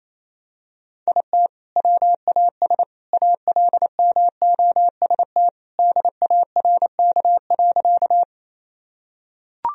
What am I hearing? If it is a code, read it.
Morse code, heard as a keyed tone, about 28 wpm, 700 Hz. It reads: IT WAS ALMOST DARK.
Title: Sentences from Top 1000 Words - Review Part 4 - 28wpm